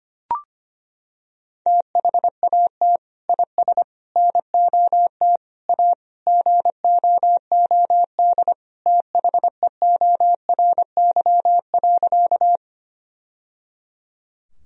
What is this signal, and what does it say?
Morse code, heard as a keyed tone, about 25 wpm, 700 Hz. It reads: THAT IS NOT A GOOD THEORY.
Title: Sentences from Top 1000 Words - Review Part 3 - 25wpm